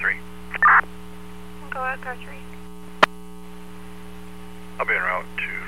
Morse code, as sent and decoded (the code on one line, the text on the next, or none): none